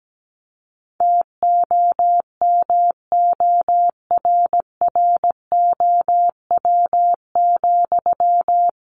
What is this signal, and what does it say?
Morse code, heard as a keyed tone, about 17 wpm, 700 Hz. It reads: TOMORROW,
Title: Sentences from Top 1500 Words 17wpm